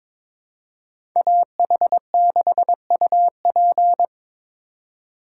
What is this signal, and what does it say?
Morse code, heard as a keyed tone, about 25 wpm, 700 Hz. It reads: AH6UP
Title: Speed Racing - USA 2x2 Call Signs 22wpm